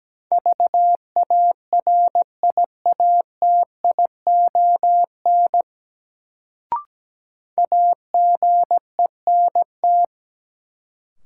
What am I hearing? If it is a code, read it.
Morse code, heard as a keyed tone, about 17 wpm, 700 Hz. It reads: VARIATION AGENT
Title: Sentences from Top 1600 Words - New Word Drill Alt - 17wpm